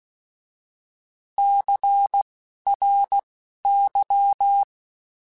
Morse code, heard as a keyed tone, about 16 wpm, spaced as for 12 wpm, 800 Hz. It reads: CRY
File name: Alphabetical / Alphabetical mp3s / Cs @10